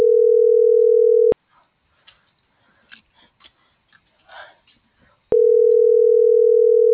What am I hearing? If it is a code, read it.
Ringback tone (North America), 440 + 480 Hz.